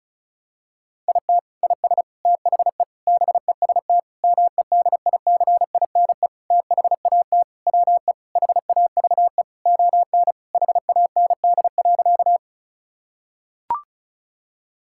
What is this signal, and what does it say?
Morse code, heard as a keyed tone, about 36 wpm, 690 Hz. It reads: IT IS THE BEST MEDICINE THAT WE HAVE ON HAND.
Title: Sentences from Top 1000 Words - Review Part 6 - 35wpm